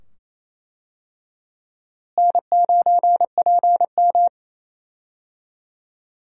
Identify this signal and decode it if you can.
Morse code, heard as a keyed tone, about 28 wpm, 700 Hz. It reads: N9PM
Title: USA Call Signs - 4 Characters - Repeated 2x - 28wpm